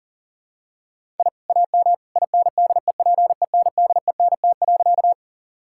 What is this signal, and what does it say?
Morse code, heard as a keyed tone, about 40 wpm, 700 Hz. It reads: I AM INDEPENDENT.
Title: Sentences from Top 1000 Words - Review Part 6 - 40wpm